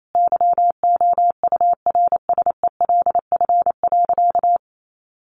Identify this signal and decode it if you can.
Morse code, heard as a keyed tone, about 28 wpm, 700 Hz. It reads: YOURSELF.
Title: Sentences from Top 600 Words 28wpm